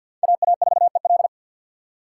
Morse code, heard as a keyed tone, about 48 wpm, 690 Hz. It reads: AA4EL
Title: USA Call Signs - 5 Characters - 3x Character Spacing - 50wpm